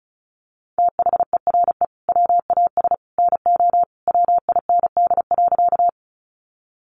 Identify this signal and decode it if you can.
Morse code, heard as a keyed tone, about 34 wpm, 700 Hz. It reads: THERE WAS NO WIND.
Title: Sentences from Top 400 Words 35wpm